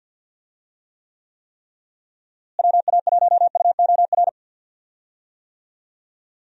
Morse code, heard as a keyed tone, about 52 wpm, 700 Hz. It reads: WA1UOR
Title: USA Call Signs - 6 Characters - Repeated 3x - 50wpm